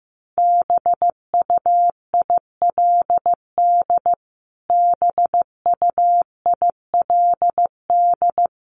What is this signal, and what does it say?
Morse code, heard as a keyed tone, about 15 wpm, 700 Hz. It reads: BUILD BUILD
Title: Top 300 Words Repeated 15wpm